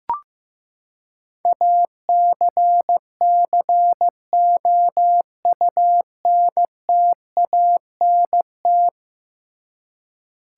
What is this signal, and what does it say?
Morse code, heard as a keyed tone, about 15 wpm, 700 Hz. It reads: ACCOUNTANT
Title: Practice QSO - Exchange 3 - More Info - 15wpm